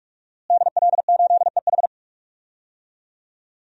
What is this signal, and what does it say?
Morse code, heard as a keyed tone, about 46 wpm, 700 Hz. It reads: DL8EH
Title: DXCC-100 Call Signs - 5 Characters - Repeated 2x - 45wpm